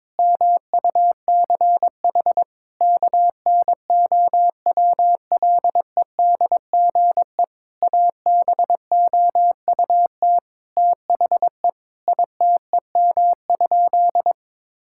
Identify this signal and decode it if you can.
Morse code, heard as a keyed tone, about 22 wpm, 700 Hz. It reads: MUCH KNOWLEDGE ABOUT THE ITEM?
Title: Sentences from Top 1000 Words - Review Part 6 - 22wpm